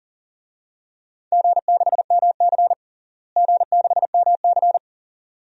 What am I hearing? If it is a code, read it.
Morse code, heard as a keyed tone, about 40 wpm, 700 Hz. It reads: G6MC G6MC
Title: DXCC-100 Call Signs - 4 Characters - Repeated 2x - 40wpm